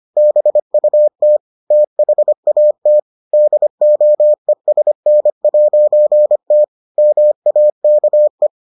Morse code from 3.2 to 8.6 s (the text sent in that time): DOESN'T MAKE